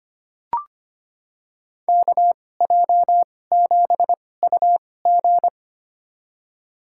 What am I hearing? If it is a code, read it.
Morse code, heard as a keyed tone, about 25 wpm, spaced as for 18 wpm, 700 Hz. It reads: KJ7UG
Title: USA Call Signs - 5 Characters - 2x Character Spacing - 25wpm